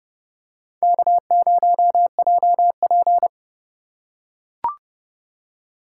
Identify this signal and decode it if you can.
Morse code, heard as a keyed tone, about 30 wpm, 690 Hz. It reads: K0JP
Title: USA Call Signs - 4 Characters - 2x Character Spacing - 30wpm